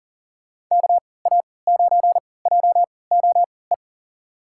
Morse code, heard as a keyed tone, about 40 wpm, spaced as for 23 wpm, 700 Hz. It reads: KA9JOE
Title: USA Call Signs - 6 Characters - 3x Character Spacing - 40wpm